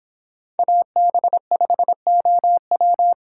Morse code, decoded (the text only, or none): AB5OW